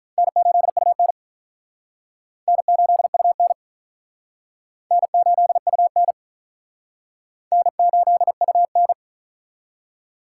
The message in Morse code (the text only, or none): N8UN N8UN N8UN N8UN